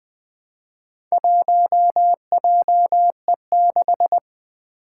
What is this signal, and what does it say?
Morse code, heard as a keyed tone, about 20 wpm, 700 Hz. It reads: 1JE6
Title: Four Challenging Letter-Number Combinations 20wpm